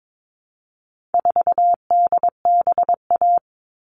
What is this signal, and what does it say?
Morse code, heard as a keyed tone, about 22 wpm, 700 Hz. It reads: 4DBA
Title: All Cut Numbers - 4-Digit 22wpm